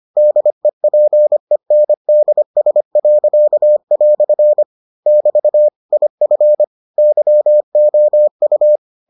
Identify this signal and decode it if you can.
Morse code, heard as a keyed tone, about 25 wpm, 600 Hz. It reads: DEPENDS." <BT> IF YOU